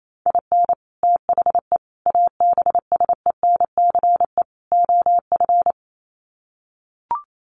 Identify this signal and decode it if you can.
Morse code, heard as a keyed tone, about 28 wpm, 700 Hz. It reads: IN THE ABSENCE OF
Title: Sets of 4 Words 28wpm